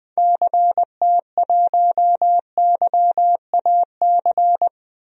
Morse code, heard as a keyed tone, about 20 wpm, 700 Hz. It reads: CT1YAC
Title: DXCC-100 Call Signs - 6 Characters - Repeated 2x - 20wpm